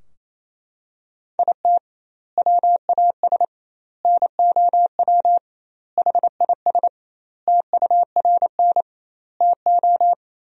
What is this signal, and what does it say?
Morse code, heard as a keyed tone, about 28 wpm, 700 Hz. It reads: IT WAS NOW HIS TURN TO
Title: 2x Word-Spacing - Sentences from Top 200 Words - 28wpm